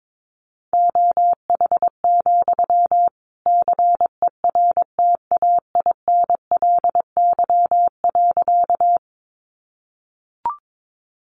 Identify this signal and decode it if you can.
Morse code, heard as a keyed tone, about 22 wpm, 700 Hz. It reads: OH, CERTAINLY.
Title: Sentences from Top 900 Words 22wpm